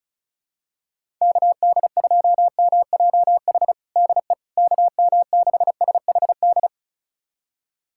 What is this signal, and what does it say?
Morse code, heard as a keyed tone, about 36 wpm, 700 Hz. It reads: KD2MJH DE KM6SHD
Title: USA Callsign DE Callsign 35wpm